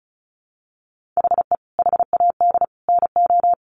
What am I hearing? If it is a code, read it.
Morse code, heard as a keyed tone, about 34 wpm, 700 Hz. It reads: HE HAD NO
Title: Sentences from Top 1400 Words 35wpm